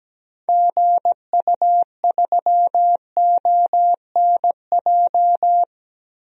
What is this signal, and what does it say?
Morse code, heard as a keyed tone, about 17 wpm, 700 Hz. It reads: GU3ONJ
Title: DXCC-100 Call Signs - 6 Characters - Repeated 2x - 17wpm